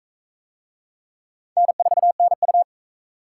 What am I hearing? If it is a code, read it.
Morse code, heard as a keyed tone, about 39 wpm, 700 Hz. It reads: N4NU